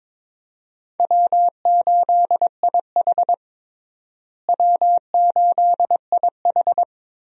Morse code, heard as a keyed tone, about 22 wpm, 700 Hz. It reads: W8IH W8IH
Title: USA Call Signs - 4 Characters - Repeated 2x - 22wpm